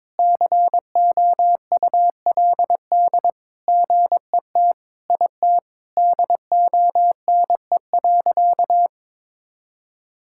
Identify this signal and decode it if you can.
Morse code, heard as a keyed tone, about 22 wpm, 700 Hz. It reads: COULD GET IT DONE.
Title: Sentences from Top 1000 Words - Review Part 5 - 22wpm